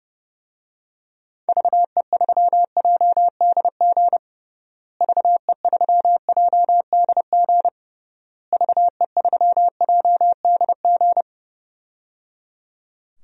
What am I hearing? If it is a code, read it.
Morse code, heard as a keyed tone, about 30 wpm, 700 Hz. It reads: VE3JDG VE3JDG VE3JDG
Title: DXCC-100 Call Signs - 6 Characters - Repeated 3x - 30wpm